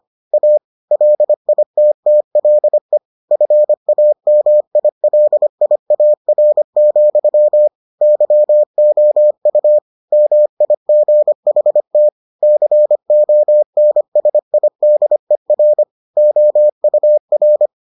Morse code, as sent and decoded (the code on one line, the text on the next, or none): .- / .-.. .. - - .-.. / ..-. .- -- .. .-.. .. .- .-. --..-- / -.-- --- ..- / -- .. --. .... - / -.-. --- -. ... .. -.. .-. / --- ..- .-.
A LITTLE FAMILIAR, YOU MIGHT CONSIDER OUR